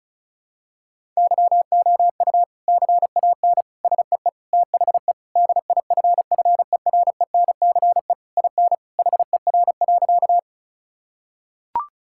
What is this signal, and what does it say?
Morse code, heard as a keyed tone, about 36 wpm, 700 Hz. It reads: YOU CAN SEE THE DIFFERENCE IN HER.
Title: Sentences from Top 1000 Words - Review Part 5 - 35wpm